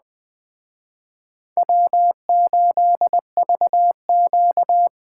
Morse code, heard as a keyed tone, about 21 wpm, 700 Hz. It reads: W8VQ
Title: Speed Racing - USA 1x2 Call Signs 17wpm